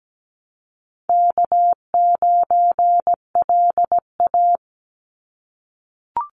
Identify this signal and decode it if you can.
Morse code, heard as a keyed tone, about 17 wpm, 700 Hz. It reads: K9LA
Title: Easy Call Signs - 17wpm